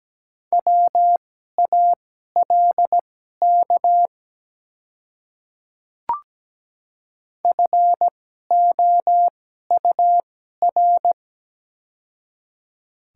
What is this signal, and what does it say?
Morse code, heard as a keyed tone, about 17 wpm, spaced as for 12 wpm, 700 Hz. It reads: WALK FOUR
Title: 2x Character Spacing - Top 300 Words - 17wpm